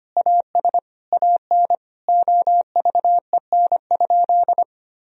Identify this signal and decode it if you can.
Morse code, heard as a keyed tone, about 25 wpm, 700 Hz. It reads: AS AN OVEN?
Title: Sentences from Top 1000 Words - Review Part 4 - 25wpm